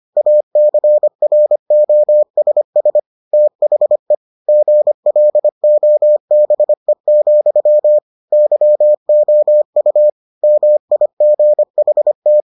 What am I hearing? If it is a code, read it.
Morse code, heard as a keyed tone, about 25 wpm, 600 Hz. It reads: ACROSS THE GLOBE, YOU MIGHT